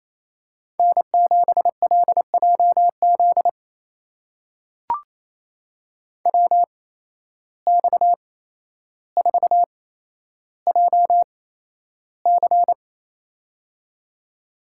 Morse code, heard as a keyed tone, about 28 wpm, spaced as for 12 wpm, 700 Hz. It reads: N7LJZ WX4JC
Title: USA Call Signs - 5 Characters - 8x Character Spacing - 28wpm